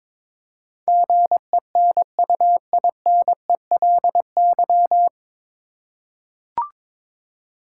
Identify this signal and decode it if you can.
Morse code, heard as a keyed tone, about 22 wpm, 700 Hz. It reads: GENUINELY
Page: Morse Code Ninja practice 22wpm